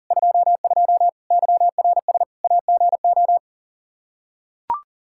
Morse code, heard as a keyed tone, about 37 wpm, 700 Hz. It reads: 22 YRS AGO
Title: Practice QSO - Exchange 3 - More Info - 40wpm